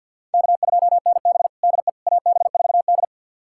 Morse code, heard as a keyed tone, about 53 wpm, 700 Hz. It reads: K2NB DE AB4D
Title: USA Callsign DE Callsign - 4 Characters - 50wpm